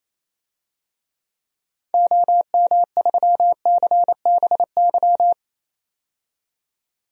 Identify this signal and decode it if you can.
Morse code, heard as a keyed tone, about 28 wpm, 700 Hz. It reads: OM3CBY
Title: DXCC-100 Call Signs - 6 Characters - 28wpm